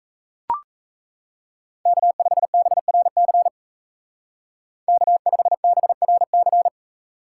Morse code, heard as a keyed tone, about 34 wpm, 700 Hz. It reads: K5BRC K5BRC
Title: Speed Racing - USA 1x3 Call Signs 28wpm